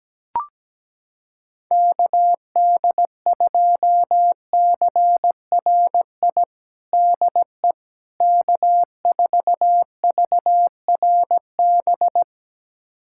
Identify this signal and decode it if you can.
Morse code, heard as a keyed tone, about 17 wpm, 700 Hz. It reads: KD2CRI DE K4VRB
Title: USA Callsign DE Callsign 17wpm